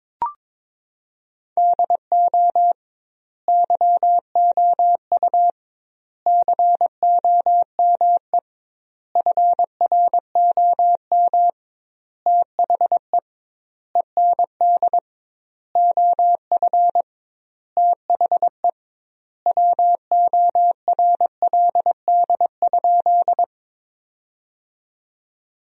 Morse code, heard as a keyed tone, about 22 wpm, 700 Hz. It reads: DO YOU COME FROM THE END OF THE WORLD?
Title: 2x Word-Spacing - Sentences from Top 200 Words - 22wpm